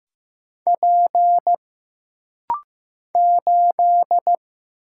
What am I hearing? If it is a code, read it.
Morse code, heard as a keyed tone, about 15 wpm, 700 Hz. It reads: P 8
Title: Single Letter-Number Alt - Rapid-Fire 15wpm